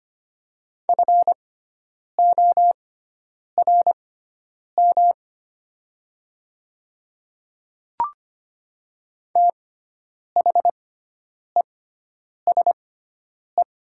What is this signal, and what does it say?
Morse code, heard as a keyed tone, about 25 wpm, spaced as for 9 wpm, 700 Hz. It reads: FORM THESE
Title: 6x Character Spacing - Top 200 Words - 25wpm